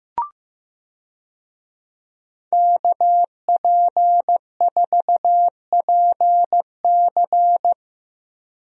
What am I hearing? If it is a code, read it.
Morse code, heard as a keyed tone, about 15 wpm, 700 Hz. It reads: KP4PC
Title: DXCC-100 Call Signs - 5 Characters - Repeated 3x - 15wpm